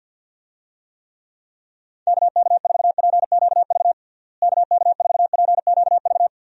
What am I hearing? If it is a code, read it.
Morse code, heard as a keyed tone, about 52 wpm, 700 Hz. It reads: KK4PQV KK4PQV
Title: USA Call Signs - 6 Characters - Repeated 2x - 50wpm